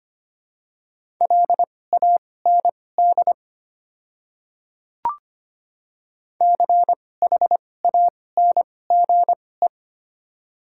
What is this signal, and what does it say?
Morse code, heard as a keyed tone, about 25 wpm, spaced as for 18 wpm, 700 Hz. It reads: LAND CHANGE
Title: 2x Character Spacing - Top 200 Words - 25wpm